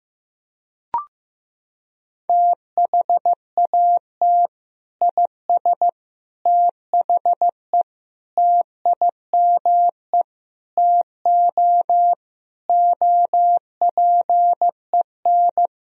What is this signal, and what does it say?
Morse code, heard as a keyed tone, about 15 wpm, 700 Hz. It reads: THAT IS THE TIME TO OPEN
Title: Sentences from Top 1000 Words - Review Part 1 - 15wpm